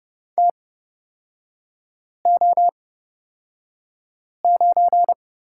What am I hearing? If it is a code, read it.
Morse code, heard as a keyed tone, about 30 wpm, spaced as for 5 wpm, 700 Hz. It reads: TO9
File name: Lesson 36e - BT - Three Characters 30wpm